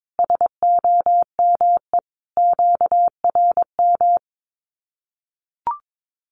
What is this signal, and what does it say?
Morse code, heard as a keyed tone, about 22 wpm, 700 Hz. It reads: SOME QRM
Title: Common QSO Phrases 22wpm